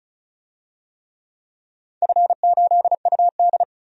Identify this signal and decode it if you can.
Morse code, heard as a keyed tone, about 34 wpm, 700 Hz. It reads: F8UD
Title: DXCC-100 Call Signs - 4 Characters - 35wpm